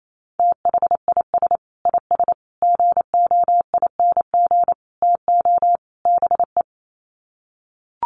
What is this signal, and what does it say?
Morse code, heard as a keyed tone, about 28 wpm, 700 Hz. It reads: THIS IS GOING TO BE
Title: Sets of 5 Words 28wpm